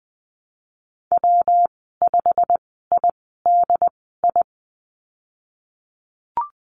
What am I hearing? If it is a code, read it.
Morse code, heard as a keyed tone, about 20 wpm, spaced as for 15 wpm, 700 Hz. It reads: W5IDI
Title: 161 Lesson 26g - Letter g - Call Signs 20wpm